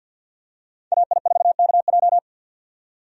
Morse code, heard as a keyed tone, about 50 wpm, 700 Hz. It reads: AI4KJ
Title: USA Call Signs - 5 Characters - 50wpm